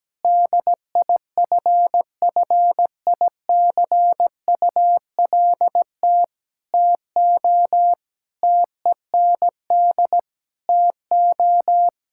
Morse code, heard as a keyed tone, about 17 wpm, 700 Hz. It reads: DIFFICULT TO TEND TO
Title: Sentences from Top 1000 Words - Review Part 5 - 17wpm